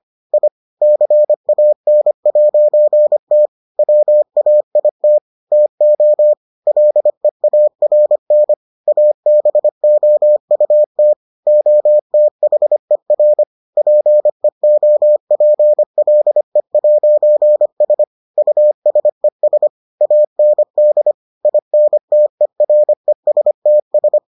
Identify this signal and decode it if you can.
Morse code, heard as a keyed tone, about 25 wpm, 600 Hz. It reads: I CAN'T WAIT TO LEARN ABOUT OTHER PEOPLE'S USES AND INTERESTS